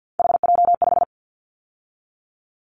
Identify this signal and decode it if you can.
Morse code, heard as a keyed tone, about 50 wpm, 700 Hz. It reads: HJ5
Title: Callsign Prefixes for Top 100 DXCC - 50wpm